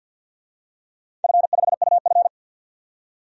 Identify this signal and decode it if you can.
Morse code, heard as a keyed tone, about 47 wpm, 700 Hz. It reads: F5UF
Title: DXCC-100 Call Signs - 4 Characters - 50wpm